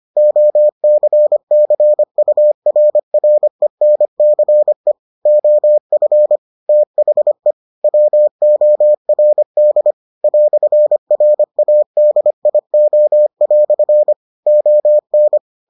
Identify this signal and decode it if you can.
Morse code, heard as a keyed tone, about 25 wpm, 600 Hz. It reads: OCCURRENCE OF THE WORD "RADIO" ON